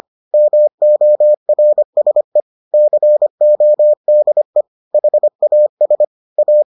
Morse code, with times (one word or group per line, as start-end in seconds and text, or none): 0.3-2.4 MORSE
2.7-4.6 CODE
4.9-6.0 HAS
6.4-6.6 A